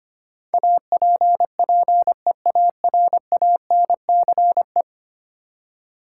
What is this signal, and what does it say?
Morse code, heard as a keyed tone, about 25 wpm, 700 Hz. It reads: APPEARANCE
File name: Ten Letter Words - 25wpm